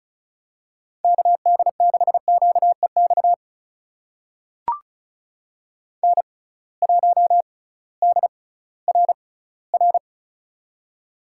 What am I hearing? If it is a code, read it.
Morse code, heard as a keyed tone, about 36 wpm, spaced as for 19 wpm, 700 Hz. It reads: KD6QEX N1DRR
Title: USA Call Signs - 6x Character Spacing - 35wpm